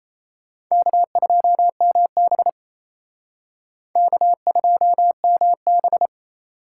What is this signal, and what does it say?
Morse code, heard as a keyed tone, about 36 wpm, 700 Hz. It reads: K2MB K2MB